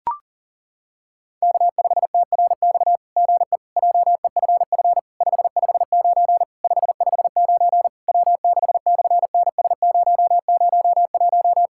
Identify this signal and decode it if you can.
Morse code, heard as a keyed tone, about 40 wpm, 700 Hz. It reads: K5TRX GE JEFF 559 559 W6/NS001
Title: SOTA QSO Practice 40wpm